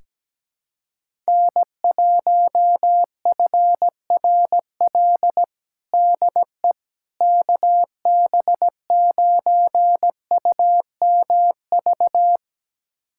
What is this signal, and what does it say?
Morse code, heard as a keyed tone, about 17 wpm, 700 Hz. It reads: N1FRL DE KB9UMV
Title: USA Callsign DE Callsign 17wpm